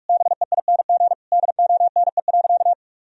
Morse code, heard as a keyed tone, about 41 wpm, 700 Hz. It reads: BEING DONE.
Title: Sentences from Top 1000 Words - Review Part 3 - 45wpm